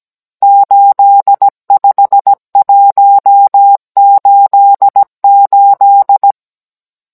Morse code, heard as a keyed tone, about 17 wpm, 800 Hz. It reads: 85188